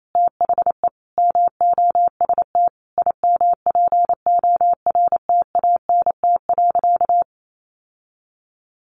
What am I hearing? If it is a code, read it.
Morse code, heard as a keyed tone, about 28 wpm, 700 Hz. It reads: THE MOST IMPORTANT.
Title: Sentences from Top 700 Words 28wpm